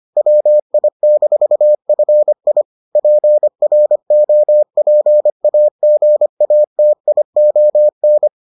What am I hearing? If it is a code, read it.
Morse code, heard as a keyed tone, about 25 wpm, 600 Hz. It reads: WI-FI PROPAGATION